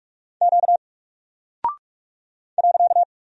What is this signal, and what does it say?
Morse code, heard as a keyed tone, about 44 wpm, 700 Hz. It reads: Q .
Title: Single Character (Letter:Number:Punctuation:Prosign) 45wpm